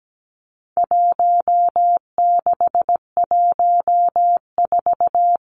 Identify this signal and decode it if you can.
Morse code, heard as a keyed tone, about 17 wpm, 700 Hz. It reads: 1614
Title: Numbers - Contest Style - 4-Digit 17wpm